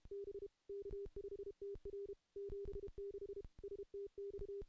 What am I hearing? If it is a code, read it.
Morse code, heard as a keyed tone, about 32 wpm, 400 Hz. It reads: B K5TR 76 STX
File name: contests / CW-SS-k5tr- -0200